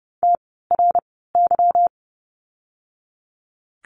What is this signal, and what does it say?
Morse code, heard as a keyed tone, about 30 wpm, spaced as for 17 wpm, 700 Hz. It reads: TRY
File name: course / 30:10wpm / 237 Lesson 38f - Letter q - Words 30wpm